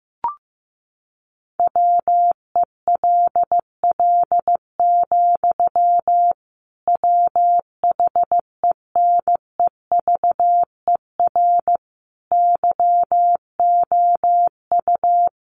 Morse code, heard as a keyed tone, about 15 wpm, 700 Hz. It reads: WELL, WHENEVER YOU
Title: Sentences from Top 1200 Words 15wpm